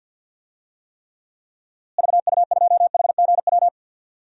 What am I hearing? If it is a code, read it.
Morse code, heard as a keyed tone, about 50 wpm, 700 Hz. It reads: VU2HGW
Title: DXCC-100 Call Signs - 6 Characters - Repeated 2x - 50wpm